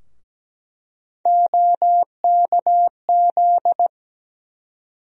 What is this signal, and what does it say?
Morse code, heard as a keyed tone, about 17 wpm, 700 Hz. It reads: OKZ
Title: USA Call Sign Postfixes - 3 Letters 17wpm